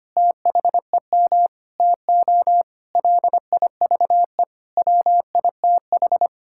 Morse code, heard as a keyed tone, about 25 wpm, 700 Hz. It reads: THEM TO LIVE WITH
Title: Sentences from Top 1600 Words 25wpm